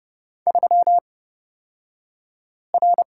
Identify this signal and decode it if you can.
Morse code, heard as a keyed tone, about 31 wpm, spaced as for 5 wpm, 700 Hz. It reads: 3R
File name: Lesson 37d - Letter x - Two Characters 30wpm